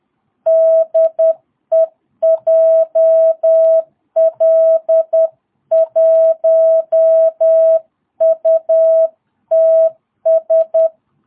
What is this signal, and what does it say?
Morse code, heard as a keyed tone, about 10 wpm, 660 Hz. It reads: DE JL1UTS
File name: jh8cbh / 2022cw